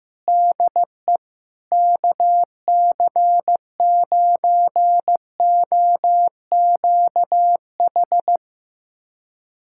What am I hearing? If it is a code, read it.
Morse code, heard as a keyed tone, about 15 wpm, 700 Hz. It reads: DE KC9OQH
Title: USA Callsign DE Callsign 15wpm